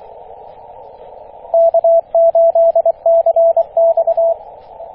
Morse code, heard as a keyed tone, about 23 wpm, 670 Hz, with radio noise. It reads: K8CX
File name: E6AC 21.030MHz CW